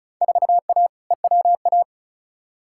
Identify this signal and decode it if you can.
Morse code, heard as a keyed tone, about 36 wpm, 700 Hz. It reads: 4A EWA
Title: Field Day Exchanges Encore - 35wpm